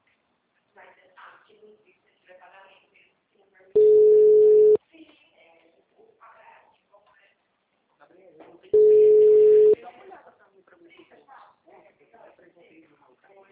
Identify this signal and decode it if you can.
Ringback tone (Europe, 425 Hz), with speech.
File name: out- -2008- - - .33954